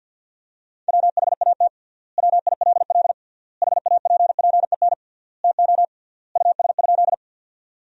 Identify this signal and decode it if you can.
Morse code, heard as a keyed tone, about 50 wpm, 700 Hz. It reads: WHAT WILL HAPPEN TO US?
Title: 3x Word-Spacing - Sentences from Top 400 Words - 50wpm